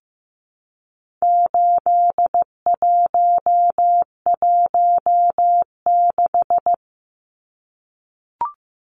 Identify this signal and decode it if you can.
Morse code, heard as a keyed tone, about 15 wpm, 700 Hz. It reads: 8116
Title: Numbers - 4-Digit 15wpm